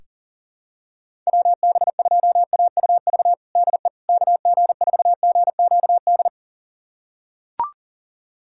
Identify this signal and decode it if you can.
Morse code, heard as a keyed tone, about 40 wpm, 700 Hz. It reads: WB2AUV DE KG4GQD
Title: USA Callsign DE Callsign - 6 Characters - 40wpm